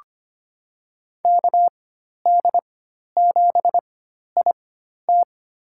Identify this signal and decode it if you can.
Morse code, heard as a keyed tone, about 25 wpm, spaced as for 12 wpm, 700 Hz. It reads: KD7IT
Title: USA Call Signs - 5 Characters - 4x Character Spacing - 25wpm